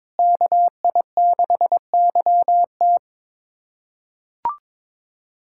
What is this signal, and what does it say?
Morse code, heard as a keyed tone, about 22 wpm, 700 Hz. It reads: KI6YT
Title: USA Callsign DE Callsign - 5 Characters - 22wpm